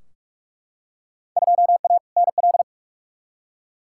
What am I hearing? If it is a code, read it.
Morse code, heard as a keyed tone, about 42 wpm, 700 Hz. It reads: 2A NL
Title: Field Day Exchanges Encore - 45wpm